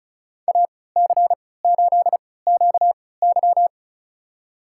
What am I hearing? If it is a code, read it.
Morse code, heard as a keyed tone, about 36 wpm, spaced as for 20 wpm, 700 Hz. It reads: AC8QY
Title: USA Call Signs - 3x Character Spacing - 35wpm